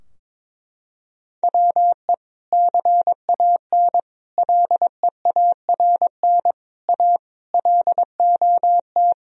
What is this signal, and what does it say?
Morse code, heard as a keyed tone, about 22 wpm, 700 Hz. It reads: WE CAN LEARN A LOT
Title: Sentences from Top 1000 Words - Review Part 2 - 22wpm